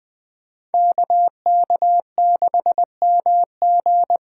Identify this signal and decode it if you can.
Morse code, heard as a keyed tone, about 20 wpm, 700 Hz. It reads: KK6MG
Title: USA Call Signs - 5 Characters - 4x Character Spacing - 20wpm